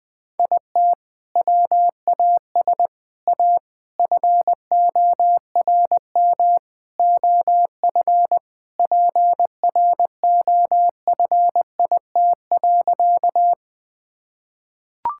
Morse code, heard as a keyed tone, about 20 wpm, 700 Hz. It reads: IT WAS A FORM OF PROFIT.
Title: Sentences from Top 1000 Words - Review Part 5 - 20wpm